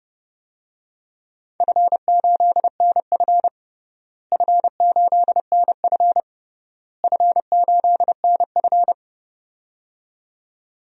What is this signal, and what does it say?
Morse code, heard as a keyed tone, about 30 wpm, 700 Hz. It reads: F8NF F8NF F8NF
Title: DXCC-100 Call Signs - 4 Characters - Repeated 3x - 30wpm